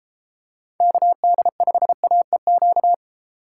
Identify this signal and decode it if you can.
Morse code, heard as a keyed tone, about 32 wpm, 700 Hz. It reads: KD5AEQ